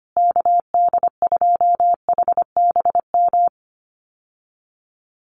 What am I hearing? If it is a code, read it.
Morse code, heard as a keyed tone, about 25 wpm, 700 Hz. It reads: KD2HBM
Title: USA Call Signs Repeated 3x 25wpm